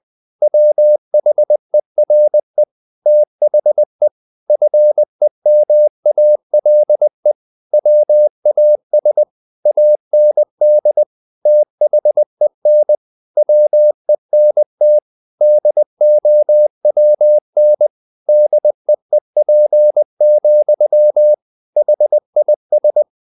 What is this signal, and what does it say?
Morse code, heard as a keyed tone, about 20 wpm, 600 Hz. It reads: WHERE THE FEMALE WAS AND THEN WENT DOWN DEEP, HIS
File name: the old man and the sea / the old man and the sea part 1 20wpm